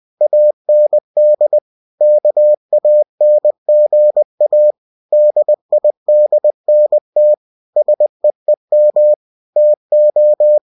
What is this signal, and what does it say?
Morse code, heard as a keyed tone, about 20 wpm, 600 Hz. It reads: AND KANGA DIDNT SEEM TO
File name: cw-books / winnie-the-pooh / winnie-20wpm-page-0071